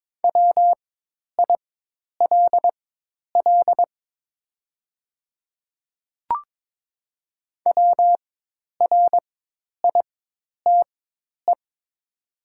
Morse code, heard as a keyed tone, about 22 wpm, spaced as for 10 wpm, 700 Hz. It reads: WILL WRITE
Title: 4x Character Spacing - Top 100 Words - 22wpm